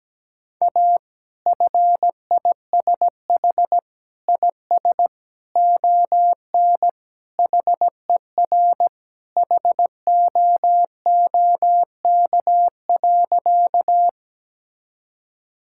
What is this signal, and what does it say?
Morse code, heard as a keyed tone, about 17 wpm, 700 Hz. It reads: A FISH IS ON HER HOOK.
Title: Sentences from Top 1600 Words 17wpm